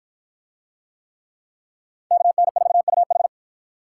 Morse code, heard as a keyed tone, about 47 wpm, 700 Hz. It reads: KN4UH